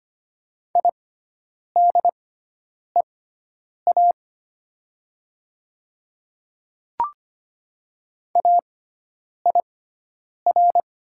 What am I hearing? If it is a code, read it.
Morse code, heard as a keyed tone, about 25 wpm, spaced as for 9 wpm, 700 Hz. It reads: IDEA AIR
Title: 6x Character Spacing - Top 300 Words - 25wpm